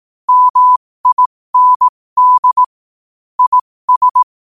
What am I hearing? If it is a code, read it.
Morse code, heard as a keyed tone, about 18 wpm, spaced as for 15 wpm, 1000 Hz. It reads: MIND IS
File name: qotdfiles / 15wpm